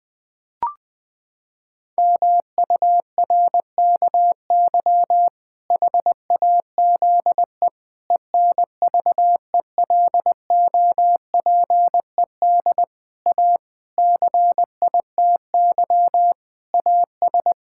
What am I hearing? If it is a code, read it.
Morse code, heard as a keyed tone, about 21 wpm, 700 Hz. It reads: MURKY HAZE ENVELOPED A CITY AS
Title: Pangram Challenge v2 - 20wpm